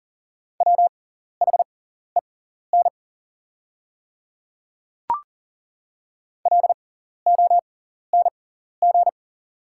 Morse code, heard as a keyed tone, about 41 wpm, spaced as for 14 wpm, 700 Hz. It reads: WHEN LONG
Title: 6x Character Spacing - Top 100 Words - 40wpm